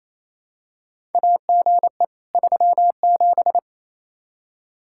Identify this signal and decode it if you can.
Morse code, heard as a keyed tone, about 28 wpm, 700 Hz. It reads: AGE 37
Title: Practice QSO - Exchange 3 - More Info - 28wpm